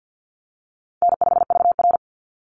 Morse code, heard as a keyed tone, about 49 wpm, 700 Hz. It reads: N5VR
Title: Easy Call Signs Repeated 2x - 50wpm